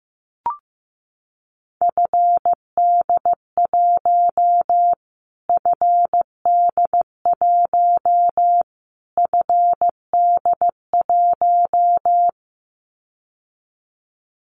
Morse code, heard as a keyed tone, about 15 wpm, 700 Hz. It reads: FD1 FD1 FD1
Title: Callsign Prefixes for Top 25 DXCC - Repeated 3x - 15wpm